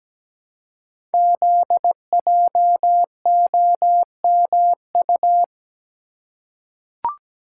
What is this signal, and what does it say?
Morse code, heard as a keyed tone, about 17 wpm, 700 Hz. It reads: ZJOMU
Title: Code Groups - 5 Letters - 17wpm